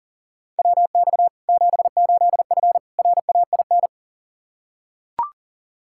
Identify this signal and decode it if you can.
Morse code, heard as a keyed tone, about 40 wpm, 700 Hz. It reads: WX 78F RAIN
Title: K1USN SST QSO Practice 40wpm